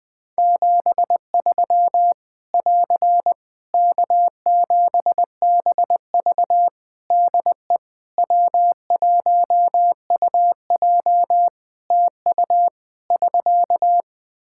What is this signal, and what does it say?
Morse code, heard as a keyed tone, about 20 wpm, 700 Hz. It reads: 73 <AR> K7BV DE W1UJ TU <SK>
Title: K1USN SST QSO Practice 20wpm